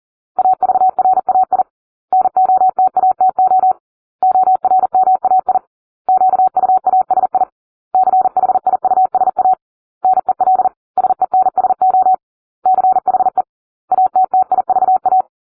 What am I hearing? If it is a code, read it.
Morse code, heard as a keyed tone, about 40 wpm, 750 Hz. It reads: AVRAI NOTATO ORMAI QUASI CHIUSA NEL SENSO CHE ATTIVA